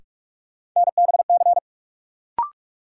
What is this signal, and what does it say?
Morse code, heard as a keyed tone, about 43 wpm, 700 Hz. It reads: NBC
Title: Sets of 5 Words Encore - New Word Drill - 45wpm